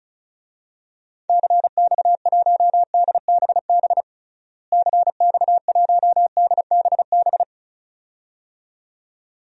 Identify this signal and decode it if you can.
Morse code, heard as a keyed tone, about 34 wpm, 700 Hz. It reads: CX1DBB CX1DBB
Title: DXCC-100 Call Signs - 6 Characters - Repeated 2x - 35wpm